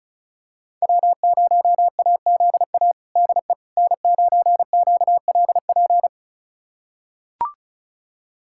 Morse code, heard as a keyed tone, about 34 wpm, 700 Hz. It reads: W0AZA DE N9QLP
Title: USA Callsign DE Callsign - 5 Characters - 35wpm